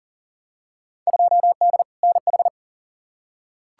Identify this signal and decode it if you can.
Morse code, heard as a keyed tone, about 40 wpm, 700 Hz. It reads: 2D NH
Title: Field Day Exchanges Encore - 40wpm